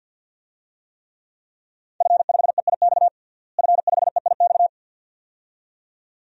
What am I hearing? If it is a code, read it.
Morse code, heard as a keyed tone, about 52 wpm, 700 Hz. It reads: F5EIX F5EIX
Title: DXCC-100 Call Signs - 5 Characters - Repeated 2x - 50wpm